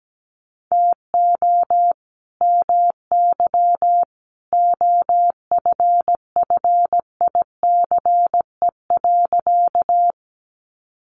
Morse code, heard as a keyed tone, about 17 wpm, 700 Hz. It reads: TO MY OFFICE.
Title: Sentences from Top 1000 Words 17wpm